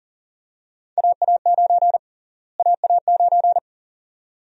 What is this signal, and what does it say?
Morse code, heard as a keyed tone, about 39 wpm, 700 Hz. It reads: AA9 AA9
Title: USA Call Sign Prefixes - Repeated 2x - 40wpm